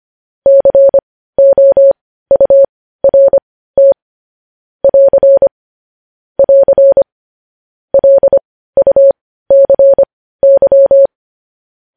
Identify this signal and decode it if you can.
Morse code, heard as a keyed tone, about 25 wpm, spaced as for 15 wpm, 550 Hz. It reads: COURT <AR> <AR> LUCY
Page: news-headlines-in-morse-code-at-15-wpm